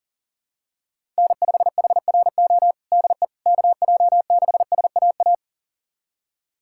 Morse code, heard as a keyed tone, about 37 wpm, 700 Hz. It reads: N5HRO DE KJ6SAA
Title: USA Callsign DE Callsign 40wpm